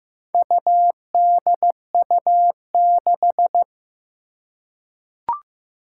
Morse code, heard as a keyed tone, about 15 wpm, 700 Hz. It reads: UDU6
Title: Four Challenging Letter-Number Combinations 15wpm